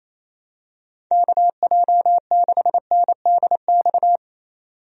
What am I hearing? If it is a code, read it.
Morse code, heard as a keyed tone, about 28 wpm, 700 Hz. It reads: KJ6NDX
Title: USA Call Signs - 6 Characters - 28wpm